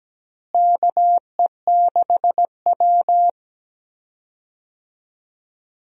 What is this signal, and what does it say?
Morse code, heard as a keyed tone, about 17 wpm, 700 Hz. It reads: KE6W
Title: USA Call Signs - 4 Characters - Repeated 2x - 17wpm